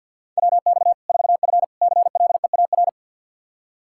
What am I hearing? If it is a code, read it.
Morse code, heard as a keyed tone, about 52 wpm, 700 Hz. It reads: WX 4F CLEAR
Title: Practice QSO - Exchange 2 - Info - 50wpm